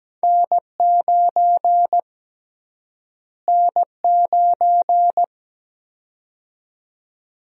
Morse code, heard as a keyed tone, about 17 wpm, 700 Hz. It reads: N9 N9
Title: USA Call Sign Prefixes - Repeated 2x - 17wpm